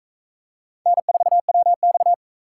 Morse code, heard as a keyed tone, about 42 wpm, 700 Hz. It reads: N4WX